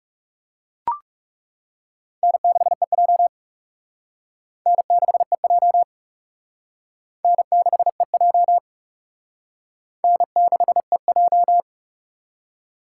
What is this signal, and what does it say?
Morse code, heard as a keyed tone, about 40 wpm, 700 Hz. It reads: N6EJ N6EJ N6EJ N6EJ